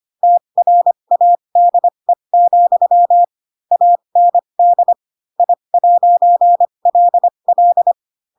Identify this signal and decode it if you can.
Morse code, heard as a keyed tone, about 25 wpm, spaced as for 22 wpm, 700 Hz. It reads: TRADE, AND I'LL